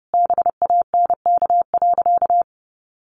Morse code, heard as a keyed tone, about 28 wpm, 700 Hz. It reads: BANK.
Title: Sentences from Top 1500 Words 30wpm